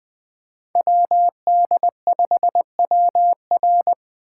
Morse code, heard as a keyed tone, about 22 wpm, 700 Hz. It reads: WD5WR
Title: Speed Racing - USA 2x2 Call Signs 20wpm